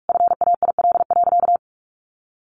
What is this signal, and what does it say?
Morse code, heard as a keyed tone, about 44 wpm, 700 Hz. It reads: FAIL.
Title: Sentences from Top 1500 Words 45wpm